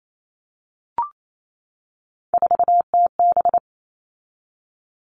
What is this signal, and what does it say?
Morse code, heard as a keyed tone, about 28 wpm, 700 Hz. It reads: 4TB
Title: All Cut Numbers - 3-Digit 28wpm